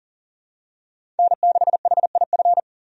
Morse code, heard as a keyed tone, about 39 wpm, 700 Hz. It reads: N6HIF
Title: USA Call Signs - 5 Characters - 40wpm